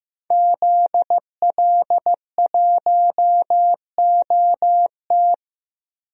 Morse code, heard as a keyed tone, about 15 wpm, 700 Hz. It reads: ZL1OT
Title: DXCC-100 Call Signs - 5 Characters - Repeated 2x - 15wpm